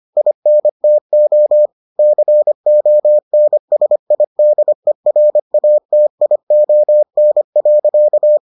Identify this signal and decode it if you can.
Morse code, heard as a keyed tone, about 25 wpm, 600 Hz. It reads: INTO CONSIDERATION.